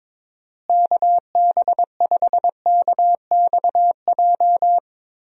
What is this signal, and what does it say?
Morse code, heard as a keyed tone, about 22 wpm, 700 Hz. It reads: KB5KXJ
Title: USA Call Signs - 6 Characters - Repeated 3x - 22wpm